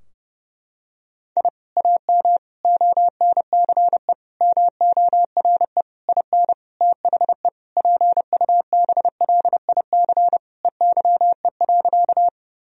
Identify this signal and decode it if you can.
Morse code, heard as a keyed tone, about 32 wpm, 700 Hz. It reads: I AM ONCE MORE IN THE PUBLIC EYE.
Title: Sentences from Top 1000 Words - Review Part 3 - 30wpm